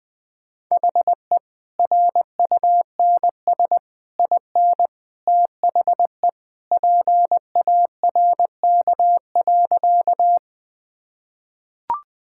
Morse code, heard as a keyed tone, about 20 wpm, 700 Hz. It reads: HE RUNS IN THE PARK.
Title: Easy Sentences with Hints - 20wpm